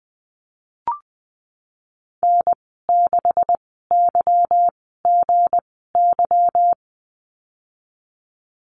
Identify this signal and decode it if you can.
Morse code, heard as a keyed tone, about 20 wpm, spaced as for 14 wpm, 700 Hz. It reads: N6YGY
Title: 179 Lesson 29g - Slash - Call Signs 20wpm